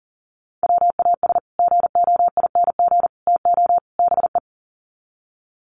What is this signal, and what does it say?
Morse code, heard as a keyed tone, about 40 wpm, 700 Hz. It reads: WAS GOING TO BE
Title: Sets of 4 Words 40wpm